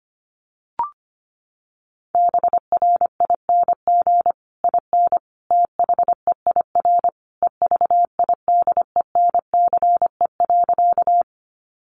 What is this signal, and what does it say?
Morse code, heard as a keyed tone, about 25 wpm, 700 Hz. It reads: BRING IN THEIR EVIDENCE.
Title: Sentences from Top 1300 Words 25wpm